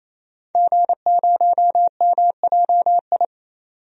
Morse code, heard as a keyed tone, about 28 wpm, 700 Hz. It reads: G0MJI
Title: DXCC-100 Call Signs - 5 Characters - 28wpm